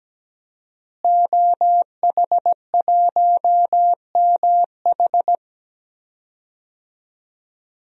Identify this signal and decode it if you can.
Morse code, heard as a keyed tone, about 17 wpm, 700 Hz. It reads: OH1MH
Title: DXCC-100 Call Signs - 5 Characters - Repeated 2x - 17wpm